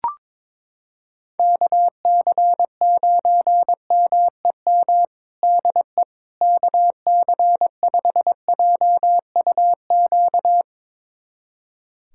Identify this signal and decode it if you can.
Morse code, heard as a keyed tone, about 22 wpm, 700 Hz. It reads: KC9MEM DE KC5JUQ
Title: USA Callsign DE Callsign - 6 Characters - 22wpm